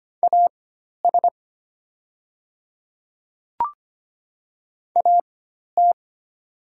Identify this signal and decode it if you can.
Morse code, heard as a keyed tone, about 25 wpm, spaced as for 12 wpm, 700 Hz. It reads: AS AT
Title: 4x Character Spacing - Top 100 Words - 25wpm